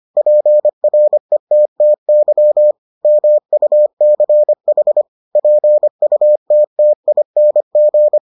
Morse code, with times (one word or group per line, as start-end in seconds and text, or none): 0.2-2.7 PRETTY
3.0-5.0 MUCH
5.4-8.2 PUTTING